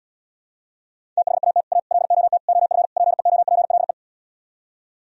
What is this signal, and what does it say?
Morse code, heard as a keyed tone, about 77 wpm, 700 Hz. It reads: THAT A LOT OF PEOPLE